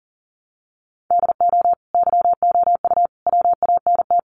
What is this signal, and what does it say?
Morse code, heard as a keyed tone, about 40 wpm, 700 Hz. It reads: DO YOU WANT
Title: Sentences from Top 1000 Words 40wpm